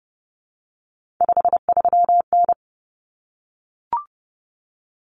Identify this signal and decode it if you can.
Morse code, heard as a keyed tone, about 30 wpm, 700 Hz. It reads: 53N
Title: Common Cut Numbers - 3-Digit 30wpm